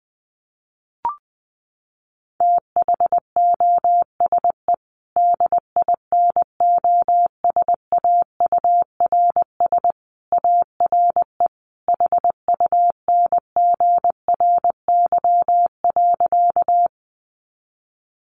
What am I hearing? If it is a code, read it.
Morse code, heard as a keyed tone, about 20 wpm, 700 Hz. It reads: THOSE DINOSAURS ARE HUNGRY.
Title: A Sound Of Thunder 20wpm